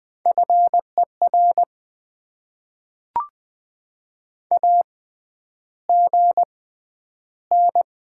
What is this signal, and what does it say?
Morse code, heard as a keyed tone, about 20 wpm, spaced as for 10 wpm, 690 Hz. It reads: FER AGN
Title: Top 20 QSO Elements - 6x Character Spacing - 20wpm